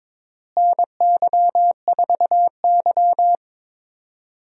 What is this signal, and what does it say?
Morse code, heard as a keyed tone, about 22 wpm, 700 Hz. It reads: NY4Y
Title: USA Call Signs - 4 Characters - Repeated 3x - 22wpm